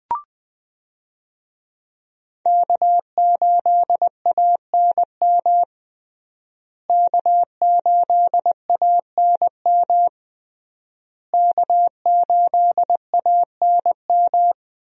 Morse code, heard as a keyed tone, about 20 wpm, 700 Hz. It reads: K8ANM K8ANM K8ANM
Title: USA Call Signs - 5 Characters - Repeated 3x - 20wpm